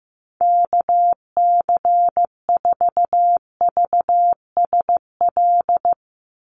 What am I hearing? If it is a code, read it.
Morse code, heard as a keyed tone, about 15 wpm, 700 Hz. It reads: KC4VSL
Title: USA Call Signs 15wpm